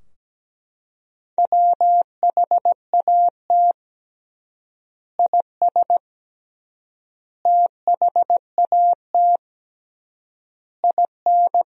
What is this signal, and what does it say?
Morse code, heard as a keyed tone, about 17 wpm, 700 Hz. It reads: WHAT IS THAT IN
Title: 3x Word-Spacing - Sentences from Top 200 Words - 17wpm